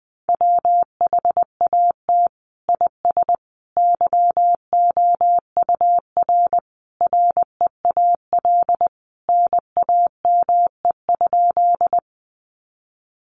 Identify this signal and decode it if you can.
Morse code, heard as a keyed tone, about 20 wpm, 700 Hz. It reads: WHAT IS YOUR REAL NAME?
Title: Sentences from Top 300 Words 20wpm